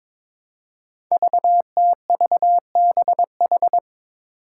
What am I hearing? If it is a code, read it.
Morse code, heard as a keyed tone, about 23 wpm, 700 Hz. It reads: VTVBH
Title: Five Challenging Letter-Number Combinations 22wpm